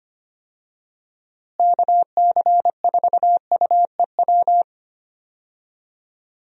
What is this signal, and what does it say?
Morse code, heard as a keyed tone, about 25 wpm, 700 Hz. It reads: KC4UEW
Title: USA Call Signs - 6 Characters - Repeated 2x - 25wpm